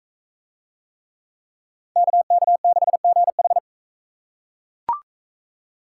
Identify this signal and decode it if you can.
Morse code, heard as a keyed tone, about 41 wpm, 700 Hz. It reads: KK6GH